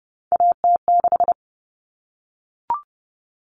Morse code, heard as a keyed tone, about 30 wpm, 700 Hz. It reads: AT6
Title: All Cut Numbers - 3-Digit 30wpm